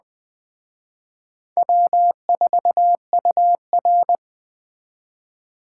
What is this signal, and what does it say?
Morse code, heard as a keyed tone, about 20 wpm, 700 Hz. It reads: W4UR